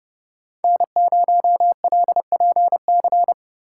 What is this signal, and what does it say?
Morse code, heard as a keyed tone, about 34 wpm, 700 Hz. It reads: N0LPC
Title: Speed Racing - USA 1x3 Call Signs 30wpm